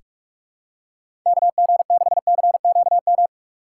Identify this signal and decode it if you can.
Morse code, heard as a keyed tone, about 44 wpm, 700 Hz. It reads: KG6CQM
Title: USA Call Signs - 6x Character Spacing - 45wpm